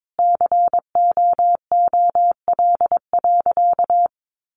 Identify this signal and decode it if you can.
Morse code, heard as a keyed tone, about 22 wpm, 700 Hz. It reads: COOL.
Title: Sentences from Top 1200 Words 22wpm